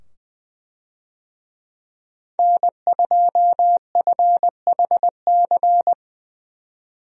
Morse code, heard as a keyed tone, about 20 wpm, 700 Hz. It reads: N2FHC
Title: USA Call Signs - 5 Characters - 20wpm